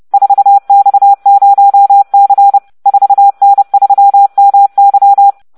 Morse code, heard as a keyed tone, about 30 wpm, 800 Hz. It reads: 4X0C 4N3MY